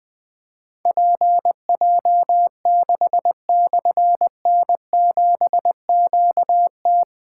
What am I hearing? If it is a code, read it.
Morse code, heard as a keyed tone, about 20 wpm, 700 Hz. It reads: PJ6/N7QT
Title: Difficult Call Signs 20wpm